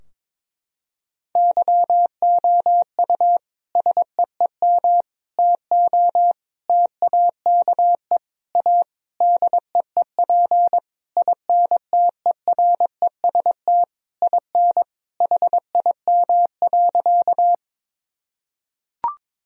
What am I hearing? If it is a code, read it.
Morse code, heard as a keyed tone, about 22 wpm, 700 Hz. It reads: YOU SEEM TO TAKE A DEEP INTEREST IN HIM.
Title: Sentences from Top 1000 Words - Review Part 3 - 22wpm